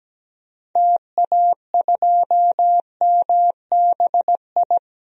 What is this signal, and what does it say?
Morse code, heard as a keyed tone, about 17 wpm, 700 Hz. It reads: TA2MBI
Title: DXCC-100 Call Signs - 6 Characters - 17wpm